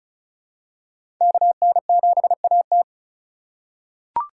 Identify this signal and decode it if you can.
Morse code, heard as a keyed tone, about 34 wpm, 700 Hz. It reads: KN7AT
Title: USA Call Signs - 5 Characters - 2x Character Spacing - 35wpm